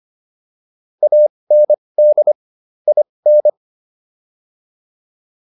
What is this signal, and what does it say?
Morse code, heard as a keyed tone, about 25 wpm, spaced as for 20 wpm, 600 Hz. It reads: AND IN